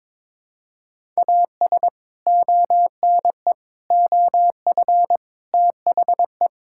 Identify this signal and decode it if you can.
Morse code, heard as a keyed tone, about 22 wpm, 700 Hz. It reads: AS ONE OF THE
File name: Build A Phrase - 5-Word Phrases - 22wpm